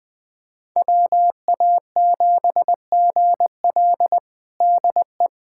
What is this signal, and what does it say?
Morse code, heard as a keyed tone, about 21 wpm, 700 Hz. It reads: WA7GL DE
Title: USA Callsign DE Callsign - 5 Characters - 20wpm